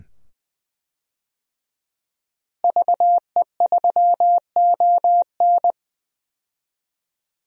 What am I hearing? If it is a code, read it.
Morse code, heard as a keyed tone, about 21 wpm, 700 Hz. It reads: VE3ON
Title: DXCC-100 Call Signs - 5 Characters - Repeated 2x - 20wpm